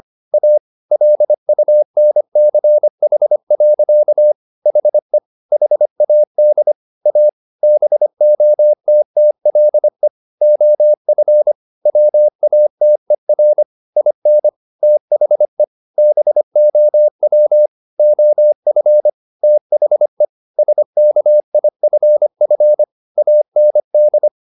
Morse code, heard as a keyed tone, about 25 wpm, 600 Hz. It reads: A LUNCH. HE HAD A BOTTLE OF WATER IN THE BOW OF THE SKIFF AND
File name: the old man and the sea part 1 25wpm